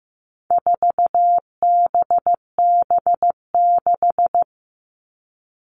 Morse code, heard as a keyed tone, about 15 wpm, 700 Hz. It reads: 4BB6
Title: All Cut Numbers - 4-Digit 15wpm